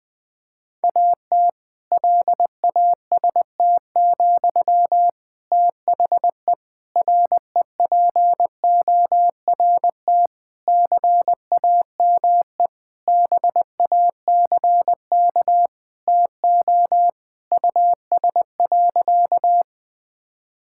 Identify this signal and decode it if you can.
Morse code, heard as a keyed tone, about 20 wpm, 700 Hz. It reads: AT LAST, THE REPORT CAME BACK TO US.
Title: Sentences from Top 1000 Words - Review Part 6 - 20wpm